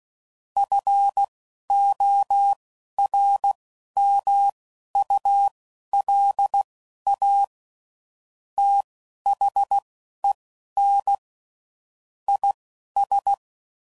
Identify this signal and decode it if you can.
Morse code, heard as a keyed tone, about 16 wpm, spaced as for 11 wpm, 800 Hz. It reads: FORMULA THEN IS
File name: CW Code Groups @10WPM / Technical Subjects / XL